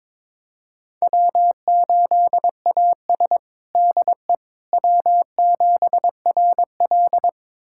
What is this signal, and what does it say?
Morse code, heard as a keyed tone, about 21 wpm, 700 Hz. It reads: W8AS DE W7RL
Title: USA Callsign DE Callsign - 4 Characters - 22wpm